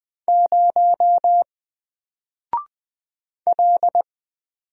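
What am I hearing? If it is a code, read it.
Morse code, heard as a keyed tone, about 20 wpm, 700 Hz. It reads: 0 L
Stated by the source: Single letter and number Morse drill at 20 wpm, from N4LCD.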